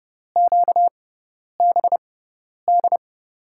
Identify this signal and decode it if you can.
Morse code, heard as a keyed tone, about 30 wpm, spaced as for 10 wpm, 700 Hz. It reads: QBD
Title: USA Call Signs - 6x Character Spacing - 30wpm